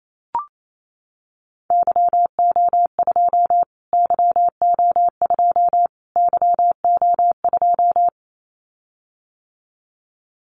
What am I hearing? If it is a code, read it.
Morse code, heard as a keyed tone, about 28 wpm, 700 Hz. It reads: YO2 YO2 YO2
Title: Callsign Prefixes for Top 25 DXCC - Repeated 3x - 28wpm